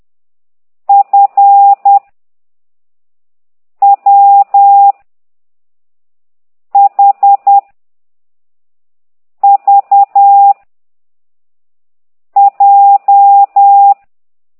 Morse code, heard as a keyed tone, about 10 wpm, 800 Hz. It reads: F W H V J
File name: ct2idw / exerc15